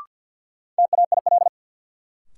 Morse code, heard as a keyed tone, about 53 wpm, 700 Hz. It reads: TAIL
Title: Sentences from Top 1000 Words - Review Part 3 - New Word Drill - 50wpm